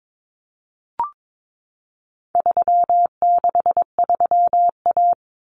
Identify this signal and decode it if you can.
Morse code, heard as a keyed tone, about 22 wpm, 700 Hz. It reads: 363A
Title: Common Cut Numbers - 4-Digit 22wpm